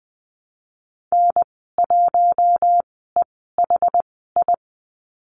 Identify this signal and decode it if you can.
Morse code, heard as a keyed tone, about 20 wpm, spaced as for 14 wpm, 700 Hz. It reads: N1EHI
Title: 213 Lesson 34g - Number 8 - Call Signs 20wpm